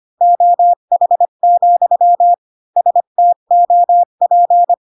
Morse code, heard as a keyed tone, about 25 wpm, 700 Hz. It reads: OH, STOP
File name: Pinocchio / Pinocchio ch1 20 25 ik1hge